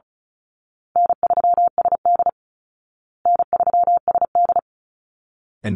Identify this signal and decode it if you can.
Morse code, heard as a keyed tone, about 34 wpm, 700 Hz. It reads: N3SD N3SD
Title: Easy Call Signs Repeated 3x - 35wpm